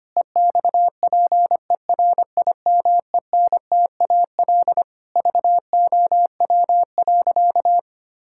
Morse code, heard as a keyed tone, about 25 wpm, 700 Hz. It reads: EXPERIMENTAL VOW.
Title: Pangram Challenge v2 - 25wpm